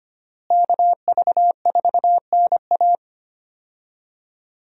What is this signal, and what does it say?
Morse code, heard as a keyed tone, about 25 wpm, 700 Hz. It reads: KV4NA
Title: USA Call Signs - 5 Characters - Repeated 2x - 25wpm